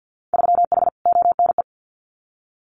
Morse code, heard as a keyed tone, about 50 wpm, 700 Hz. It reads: 3H ONE
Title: Winter Field Day Exchanges Encore 50wpm